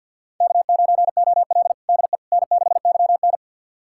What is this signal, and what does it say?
Morse code, heard as a keyed tone, about 52 wpm, 700 Hz. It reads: K9OL DE N6YN
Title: USA Callsign DE Callsign - 4 Characters - 50wpm